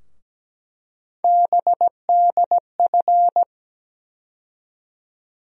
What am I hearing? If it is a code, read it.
Morse code, heard as a keyed tone, about 17 wpm, 700 Hz. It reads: BDF
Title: USA Call Sign Postfixes - 3 Letters 17wpm